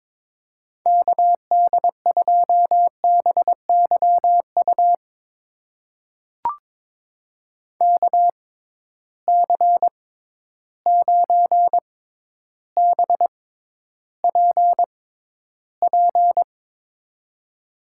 Morse code, heard as a keyed tone, about 22 wpm, spaced as for 11 wpm, 700 Hz. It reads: KD2BYU KC9BPP
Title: USA Call Signs - 6 Characters - 6x Character Spacing - 22wpm